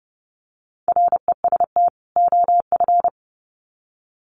Morse code, heard as a keyed tone, about 30 wpm, 700 Hz. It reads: REST OF
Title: Sets of 2 Words 30wpm